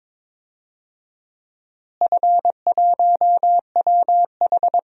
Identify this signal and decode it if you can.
Morse code, heard as a keyed tone, about 22 wpm, 700 Hz. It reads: F1WH
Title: DXCC-100 Call Signs - 4 Characters - Repeated 3x - 22wpm